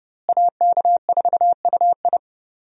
Morse code, heard as a keyed tone, about 30 wpm, 700 Hz. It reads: AK4UI
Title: USA Call Signs - 5 Characters - Repeated 3x - 30wpm